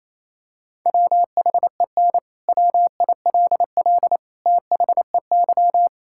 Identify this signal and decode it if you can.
Morse code, heard as a keyed tone, about 27 wpm, 700 Hz. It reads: WHEN WILL THEY
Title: Sentences from Top 1000 Words - Review Part 6 - 28wpm